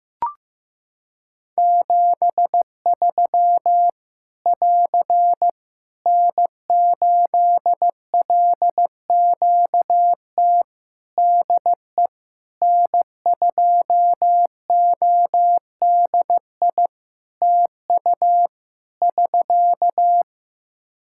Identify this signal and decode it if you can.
Morse code, heard as a keyed tone, about 15 wpm, 700 Hz. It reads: 73 <AR> N8LQT DE N2ODI TU <SK>
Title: K1USN SST QSO Practice 15wpm